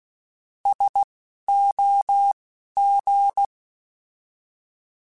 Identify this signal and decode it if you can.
Morse code, heard as a keyed tone, about 16 wpm, spaced as for 11 wpm, 800 Hz. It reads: SOG
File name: Alphabetical / Alphabetical mp3s / Ss @10